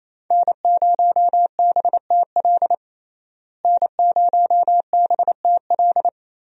Morse code, heard as a keyed tone, about 28 wpm, 700 Hz. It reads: N0BTL N0BTL
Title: USA Call Signs - 5 Characters - Repeated 3x - 28wpm